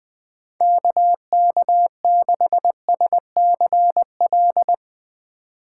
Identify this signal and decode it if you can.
Morse code, heard as a keyed tone, about 20 wpm, 700 Hz. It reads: KK6SCL